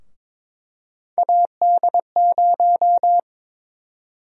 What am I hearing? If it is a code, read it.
Morse code, heard as a keyed tone, about 22 wpm, 700 Hz. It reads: AD0
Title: USA Call Sign Prefixes - Repeated 2x - 22wpm